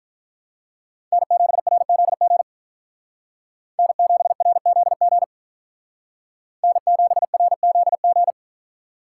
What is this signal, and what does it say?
Morse code, heard as a keyed tone, about 44 wpm, 700 Hz. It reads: N7RZG N7RZG N7RZG
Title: Speed Racing - USA 1x3 Call Signs 35wpm